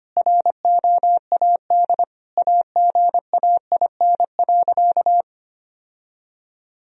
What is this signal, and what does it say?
Morse code, heard as a keyed tone, about 25 wpm, 700 Hz. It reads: ROAD AGAIN.
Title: Sentences from Top 1000 Words - Review Part 2 - 25wpm